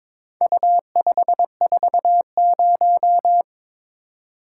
Morse code, heard as a keyed tone, about 23 wpm, 700 Hz. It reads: U540
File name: practice / 22wpm / Four Letter-Number Combinations - 22wpm